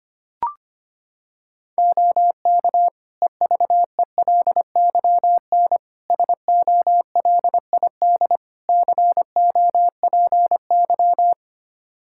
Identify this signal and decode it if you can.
Morse code, heard as a keyed tone, about 25 wpm, 700 Hz. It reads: OK EVELYN SOLID COPY
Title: Practice QSO - Exchange 3 - More Info - 25wpm